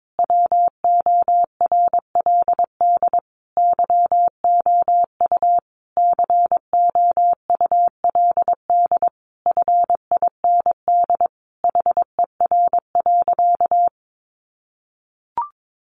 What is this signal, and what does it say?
Morse code, heard as a keyed tone, about 22 wpm, 700 Hz. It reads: WORLD YOU COULD FIND HER.
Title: Sentences from Top 1400 Words 22wpm